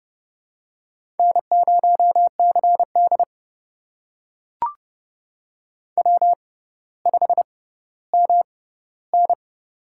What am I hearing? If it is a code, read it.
Morse code, heard as a keyed tone, about 31 wpm, spaced as for 16 wpm, 700 Hz. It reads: N0CD W5MN
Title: USA Call Signs - 6x Character Spacing - 30wpm